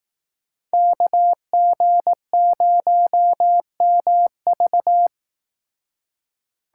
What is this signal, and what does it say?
Morse code, heard as a keyed tone, about 18 wpm, 700 Hz. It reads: KG0MV
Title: Speed Racing - USA 2x2 Call Signs 15wpm